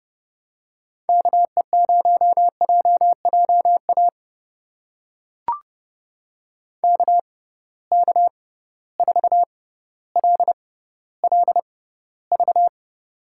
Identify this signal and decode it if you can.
Morse code, heard as a keyed tone, about 30 wpm, spaced as for 15 wpm, 700 Hz. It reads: KE0JJA KK4LLV
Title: USA Call Signs - 6 Characters - 6x Character Spacing - 30wpm